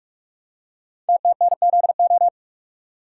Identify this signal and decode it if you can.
Morse code, heard as a keyed tone, about 48 wpm, 700 Hz. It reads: TTNZO
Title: Code Groups - 5 Letters - 45wpm